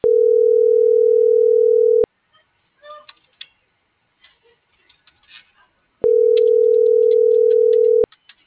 Ringback tone (North America), 440 + 480 Hz.